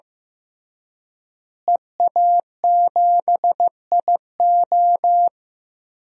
Morse code, heard as a keyed tone, about 15 wpm, 700 Hz. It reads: EA7IO
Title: DXCC-100 Call Signs - 5 Characters - Repeated 3x - 15wpm